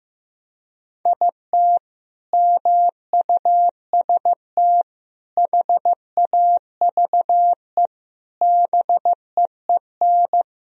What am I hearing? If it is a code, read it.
Morse code, heard as a keyed tone, about 15 wpm, 700 Hz. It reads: IT MUST HAVE BEEN
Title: Sentences from Top 1000 Words - Review Part 1 - 15wpm